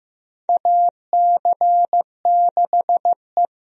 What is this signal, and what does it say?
Morse code, heard as a keyed tone, about 15 wpm, 700 Hz. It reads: AC6E
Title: USA Call Signs - 4 Characters - Repeated 2x - 15wpm